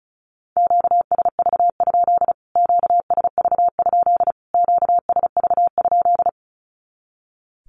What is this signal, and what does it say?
Morse code, heard as a keyed tone, about 34 wpm, 700 Hz. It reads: QSV? QSV? QSV?
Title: Q-Codes Repeated 3x 35wpm